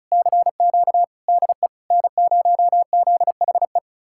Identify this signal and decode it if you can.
Morse code, heard as a keyed tone, about 47 wpm, spaced as for 42 wpm, 700 Hz. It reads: CQ DE N0ZHE
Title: Speed-Racing Common QSO Phrases - 35wpm